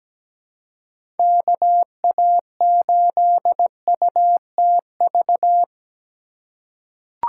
Morse code, heard as a keyed tone, about 17 wpm, 700 Hz. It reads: KA8UTV
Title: USA Call Signs - 6 Characters - 4x Character Spacing - 17wpm